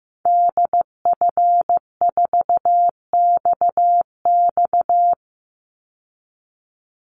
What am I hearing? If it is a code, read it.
Morse code, heard as a keyed tone, about 15 wpm, 700 Hz. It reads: DF4XX
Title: Call Signs Repeated 3x - 15wpm